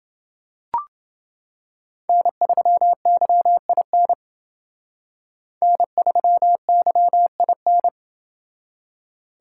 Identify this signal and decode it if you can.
Morse code, heard as a keyed tone, about 29 wpm, 700 Hz. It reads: N3YIN N3YIN